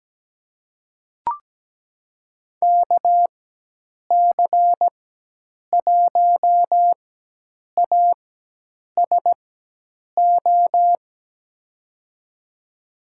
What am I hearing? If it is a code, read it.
Morse code, heard as a keyed tone, about 17 wpm, spaced as for 8 wpm, 700 Hz. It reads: KC1ASO
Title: USA Call Signs - 4x Character Spacing - 17wpm